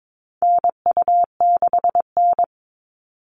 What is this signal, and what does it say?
Morse code, heard as a keyed tone, about 22 wpm, 700 Hz. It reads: NU6N
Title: Call Signs Repeated 3x - 22wpm